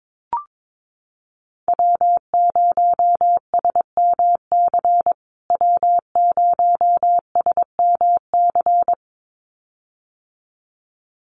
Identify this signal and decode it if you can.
Morse code, heard as a keyed tone, about 22 wpm, 700 Hz. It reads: W0SMC W0SMC
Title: USA Call Signs Repeated 2x 22wpm